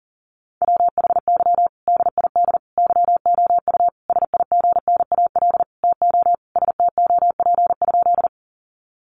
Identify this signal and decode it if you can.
Morse code, heard as a keyed tone, about 40 wpm, 700 Hz. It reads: WHY DID YOU SIGNAL TO STOP?
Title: Sentences from Top 1400 Words 40wpm